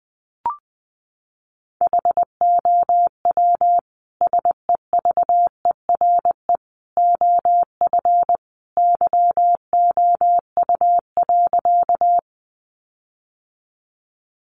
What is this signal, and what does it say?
Morse code, heard as a keyed tone, about 20 wpm, 700 Hz. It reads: HOW SEVERE OF YOU.
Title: Sentences from Top 1400 Words 20wpm